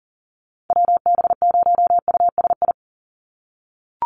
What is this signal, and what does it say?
Morse code, heard as a keyed tone, about 40 wpm, 700 Hz. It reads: WB0USI
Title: NA SOTA Chaser Callsigns 40wpm